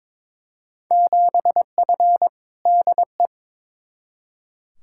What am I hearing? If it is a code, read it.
Morse code, heard as a keyed tone, about 21 wpm, 700 Hz. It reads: 7F DE
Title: Field Day Exchanges Encore - 22wpm